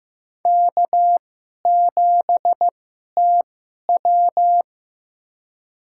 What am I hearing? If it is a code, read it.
Morse code, heard as a keyed tone, about 15 wpm, spaced as for 11 wpm, 700 Hz. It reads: K7TW
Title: USA Call Signs - 4 Characters - 2x Character Spacing - 15wpm